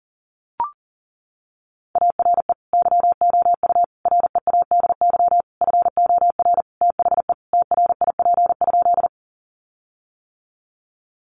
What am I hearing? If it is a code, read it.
Morse code, heard as a keyed tone, about 40 wpm, 700 Hz. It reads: ARE YOU READY FOR THE TRIP?
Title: Sentences from Top 1300 Words 40wpm